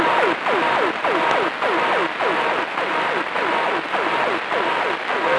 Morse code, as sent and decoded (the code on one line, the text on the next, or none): none